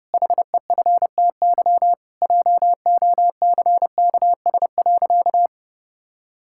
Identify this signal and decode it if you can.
Morse code, heard as a keyed tone, about 30 wpm, 700 Hz. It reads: HEFTY JOCKS.
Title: Pangram Challenge v2 - 30wpm